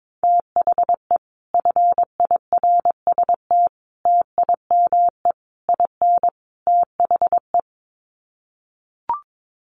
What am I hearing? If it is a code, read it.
Morse code, heard as a keyed tone, about 22 wpm, 700 Hz. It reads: THE FIRST TIME IN THE
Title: Sets of 5 Words Encore 22wpm